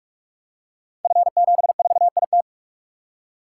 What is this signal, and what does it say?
Morse code, heard as a keyed tone, about 44 wpm, 700 Hz. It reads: F74IT
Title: Code Groups - 5 Letter:Numbers - 45wpm